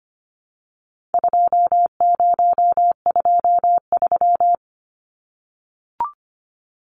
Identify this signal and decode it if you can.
Morse code, heard as a keyed tone, about 25 wpm, 700 Hz. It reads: 2023
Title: Numbers - 4-Digit 25wpm